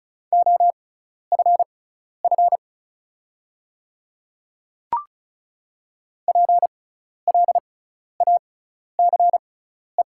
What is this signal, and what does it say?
Morse code, heard as a keyed tone, about 34 wpm, spaced as for 12 wpm, 690 Hz. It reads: OFF PLACE
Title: 6x Character Spacing - Top 300 Words - 35wpm